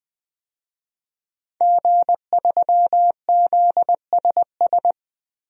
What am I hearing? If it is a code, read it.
Morse code, heard as a keyed tone, about 21 wpm, 700 Hz. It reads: G3ZSS
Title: DXCC-100 Call Signs - 5 Characters - 20wpm